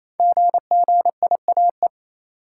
Morse code, heard as a keyed tone, about 28 wpm, 700 Hz. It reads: GGIAE
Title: Code Groups - 5 Letters - 28wpm